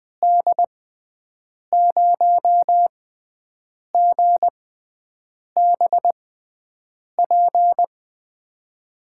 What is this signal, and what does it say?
Morse code, heard as a keyed tone, about 21 wpm, spaced as for 7 wpm, 700 Hz. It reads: D0GBP
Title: USA Call Signs - 6 Characters - 6x Character Spacing - 20wpm